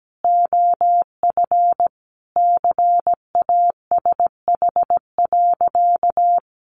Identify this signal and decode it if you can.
Morse code, heard as a keyed tone, about 17 wpm, 700 Hz. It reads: OF CASH.
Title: Sentences from Top 1300 Words 17wpm